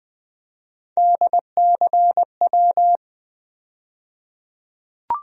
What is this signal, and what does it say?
Morse code, heard as a keyed tone, about 21 wpm, 700 Hz. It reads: DCW
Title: USA Call Sign Postfixes - 3 Letters 20wpm